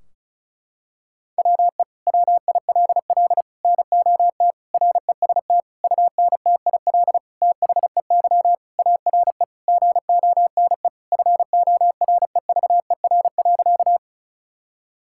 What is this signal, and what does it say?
Morse code, heard as a keyed tone, about 34 wpm, 700 Hz. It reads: WE WILL NOT REST UNTIL THEY ARE GONE FOREVER.
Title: Sentences from Top 1600 Words 35wpm